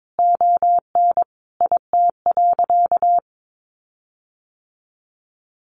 Morse code, heard as a keyed tone, about 22 wpm, 700 Hz. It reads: ON IT.
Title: Sentences from Top 1300 Words 22wpm